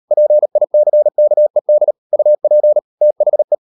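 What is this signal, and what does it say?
Morse code, heard as a keyed tone, about 38 wpm, 600 Hz. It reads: PICKED UP THE
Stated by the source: Morse code practice at 38 WPM, sending Ken Follett's novel Never in English, from CW med Gnister.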